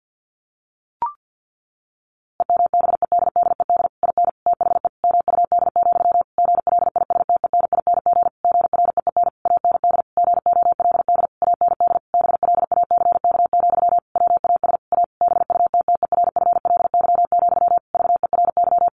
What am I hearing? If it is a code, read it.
Morse code, heard as a keyed tone, about 50 wpm, 700 Hz. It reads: EMBEDDED IN THE MUD, GLISTENING GREEN AND GOLD AND BLACK, WAS A BUTTERFLY, VERY
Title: A Sound Of Thunder 50wpm